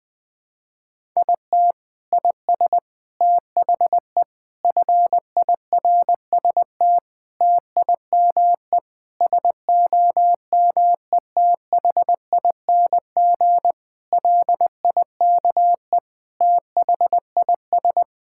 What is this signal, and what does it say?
Morse code, heard as a keyed tone, about 20 wpm, 700 Hz. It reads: IT IS THE FIRST TIME SOMETHING LIKE THIS
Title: Sentences with 8-Word Phrases - 20wpm